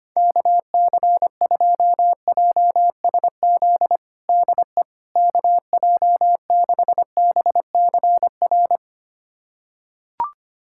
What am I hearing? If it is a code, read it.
Morse code, heard as a keyed tone, about 25 wpm, 700 Hz. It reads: KC2JSZ DE KJ6BCR
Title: USA Callsign DE Callsign - 6 Characters - 25wpm